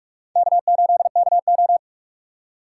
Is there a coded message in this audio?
Yes: Morse code from 0.4 to 1.8 s, heard as a keyed tone, about 44 wpm, 700 Hz. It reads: K8KO